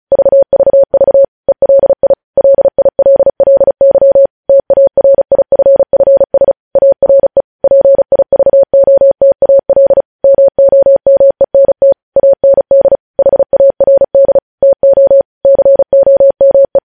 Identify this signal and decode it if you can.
Morse code, heard as a keyed tone, about 36 wpm, 550 Hz. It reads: VVV ELI LILLY TARIFFS ARE PIVOTAL MOMENT AND HARD TO COME
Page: news headlines in morse code at 35 wpm